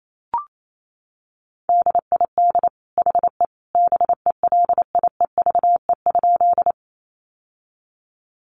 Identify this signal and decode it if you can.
Morse code, heard as a keyed tone, about 28 wpm, 700 Hz. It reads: DID HE BELIEVE?
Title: Sentences from Top 1000 Words 28wpm